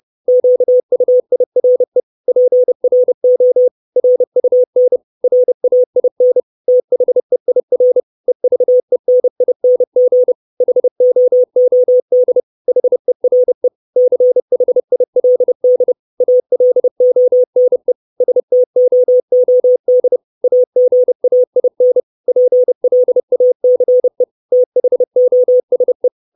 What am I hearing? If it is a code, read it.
Morse code, heard as a keyed tone, about 30 wpm, 500 Hz. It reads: QUIRE PRO RUN RAIN THEIR EVENING HOOD HERE CHILD ALONE STOOD AGAIN PLACE THOSE
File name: 632 Words 30WPM